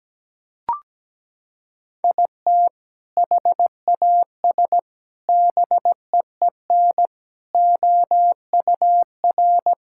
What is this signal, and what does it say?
Morse code, heard as a keyed tone, about 17 wpm, 700 Hz. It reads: IT HAS BEEN OUR
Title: Sentences from Top 1000 Words - Review Part 5 - 17wpm